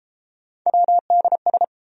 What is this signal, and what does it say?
Morse code, heard as a keyed tone, about 31 wpm, 700 Hz. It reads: WDS